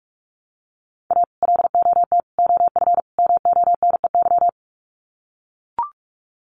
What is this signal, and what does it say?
Morse code, heard as a keyed tone, about 44 wpm, 700 Hz. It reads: A LOT OF MONEY
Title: Sets of 4 Words 45wpm